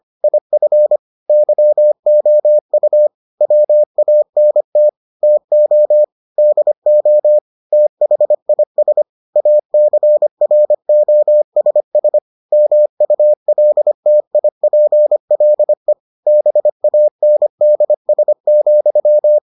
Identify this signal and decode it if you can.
Morse code, heard as a keyed tone, about 25 wpm, 600 Hz. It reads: IF YOU WANT TO DO THIS ACROSS MULTIPLE BANDS,